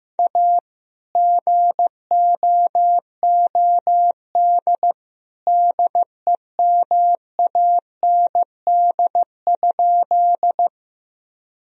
Morse code, heard as a keyed tone, about 15 wpm, 700 Hz. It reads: A GOOD DEMAND?
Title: Sentences from Top 1000 Words - Review Part 4 - 15wpm